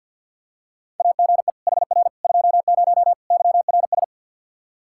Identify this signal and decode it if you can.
Morse code, heard as a keyed tone, about 47 wpm, 700 Hz. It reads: AGE HR 20 YRS
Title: Practice QSO - Exchange 3 - More Info - 50wpm